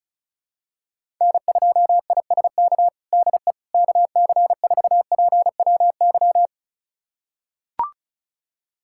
Morse code, heard as a keyed tone, about 36 wpm, 700 Hz. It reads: N2ISK DE KC4PWY
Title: USA Callsign DE Callsign 35wpm